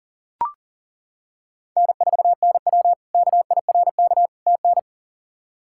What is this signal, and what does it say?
Morse code, heard as a keyed tone, about 40 wpm, 700 Hz. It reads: N4NW KIRK TN
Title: K1USN SST QSO Practice - 40wpm